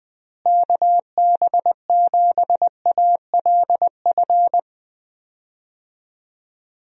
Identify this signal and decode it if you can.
Morse code, heard as a keyed tone, about 20 wpm, 700 Hz. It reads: KB7ALF
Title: USA Call Signs - 6 Characters - Repeated 3x - 20wpm